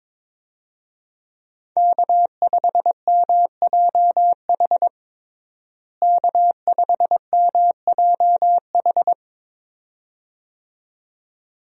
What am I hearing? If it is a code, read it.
Morse code, heard as a keyed tone, about 22 wpm, 700 Hz. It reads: K5MJH K5MJH
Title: USA Call Signs - 5 Characters - Repeated 2x - 22wpm